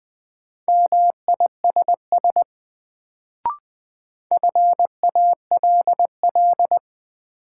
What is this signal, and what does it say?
Morse code, heard as a keyed tone, about 21 wpm, 700 Hz. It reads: MISS FALL
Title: Sentences from Top 500 Words - New Word Drill - 20wpm